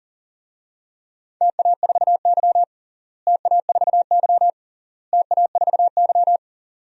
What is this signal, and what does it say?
Morse code, heard as a keyed tone, about 40 wpm, 700 Hz. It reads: TA4Y TA4Y TA4Y
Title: DXCC-100 Call Signs - 4 Characters - Repeated 3x - 40wpm